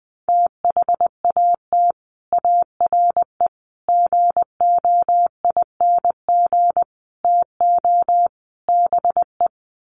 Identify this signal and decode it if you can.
Morse code, heard as a keyed tone, about 20 wpm, 700 Hz. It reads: THAT ARE GOING TO BE
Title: Sets of 5 Words Encore2 20wpm